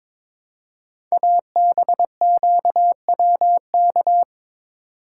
Morse code, heard as a keyed tone, about 23 wpm, 700 Hz. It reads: ABQWK
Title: Code Groups - 5 Letters - 22wpm